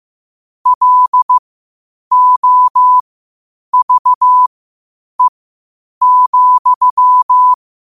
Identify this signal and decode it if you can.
Morse code, heard as a keyed tone, about 15 wpm, spaced as for 8 wpm, 1000 Hz. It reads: LOVE,
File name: qotdfiles / 8wpm